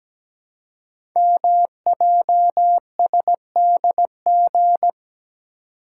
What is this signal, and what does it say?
Morse code, heard as a keyed tone, about 17 wpm, 700 Hz. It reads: MJSDG
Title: Code Groups - 5 Letters - 17wpm